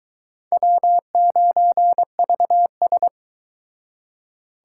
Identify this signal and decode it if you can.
Morse code, heard as a keyed tone, about 22 wpm, 700 Hz. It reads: W9VS